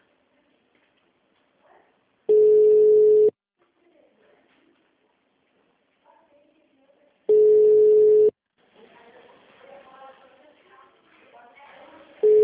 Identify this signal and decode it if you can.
Ringback tone (Europe, 425 Hz), with speech.